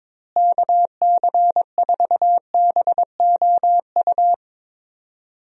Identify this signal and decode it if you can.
Morse code, heard as a keyed tone, about 22 wpm, 700 Hz. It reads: KC4BOU
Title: USA Call Signs - 6 Characters - Repeated 2x - 22wpm